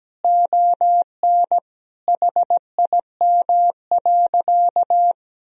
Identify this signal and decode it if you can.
Morse code, heard as a keyed tone, about 17 wpm, 700 Hz. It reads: ON HIM.
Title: Sentences from Top 1000 Words - Review Part 1 - 17wpm